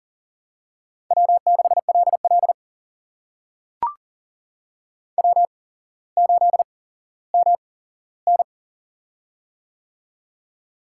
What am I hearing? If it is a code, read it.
Morse code, heard as a keyed tone, about 40 wpm, spaced as for 17 wpm, 700 Hz. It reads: W6LL W8MN
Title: USA Call Signs - 4 Characters - 8x Character Spacing - 40wpm